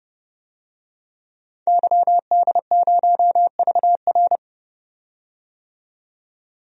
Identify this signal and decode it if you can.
Morse code, heard as a keyed tone, about 30 wpm, 700 Hz. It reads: YD0VR
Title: DXCC-100 Call Signs - 5 Characters - 30wpm